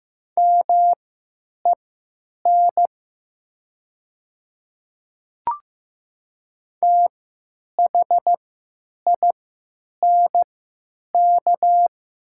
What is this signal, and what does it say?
Morse code, heard as a keyed tone, about 15 wpm, spaced as for 9 wpm, 700 Hz. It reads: MEN THINK
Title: 3x Character Spacing - Top 200 Words - 15wpm